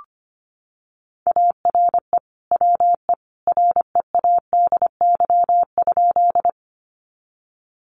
Morse code, heard as a keyed tone, about 25 wpm, 700 Hz. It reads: ARE WE READY?
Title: Sentences from Top 400 Words 25wpm